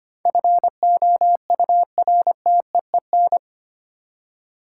Morse code, heard as a keyed tone, about 25 wpm, 700 Hz. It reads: FOURTEEN
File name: Eight Letter Words - 25wpm